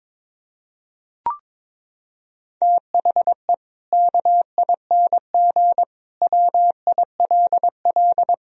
Morse code, heard as a keyed tone, about 22 wpm, 700 Hz. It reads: THE KING WILL
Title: Sentences from Top 1000 Words - Review Part 4 - 22wpm